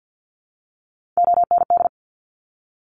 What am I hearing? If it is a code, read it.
Morse code, heard as a keyed tone, about 50 wpm, 700 Hz. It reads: OND
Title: Top Three Letter Combinations in Words 50wpm